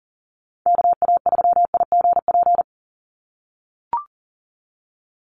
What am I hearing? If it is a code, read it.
Morse code, heard as a keyed tone, about 37 wpm, 700 Hz. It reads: KA3IGP
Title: USA Call Signs 40wpm